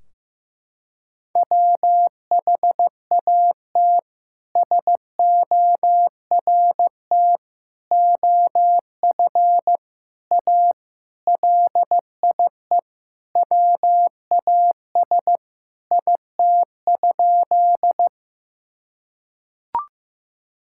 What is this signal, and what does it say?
Morse code, heard as a keyed tone, about 15 wpm, 700 Hz. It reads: WHAT SORT OF A LIE WAS IT?
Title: Sentences from Top 1600 Words 15wpm